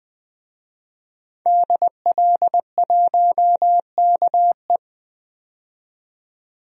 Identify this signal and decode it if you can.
Morse code, heard as a keyed tone, about 20 wpm, 700 Hz. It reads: DL1KE
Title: DXCC-100 Call Signs - 5 Characters - 20wpm